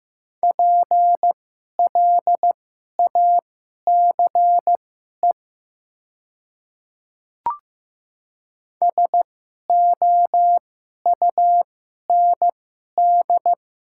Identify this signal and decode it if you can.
Morse code, heard as a keyed tone, about 15 wpm, spaced as for 11 wpm, 700 Hz. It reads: PLACE SOUND
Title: 2x Character Spacing - Top 200 Words - 15wpm